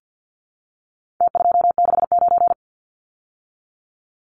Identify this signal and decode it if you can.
Morse code, heard as a keyed tone, about 50 wpm, 700 Hz. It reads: T269
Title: Numbers - Contest Style - 4-Digit 50wpm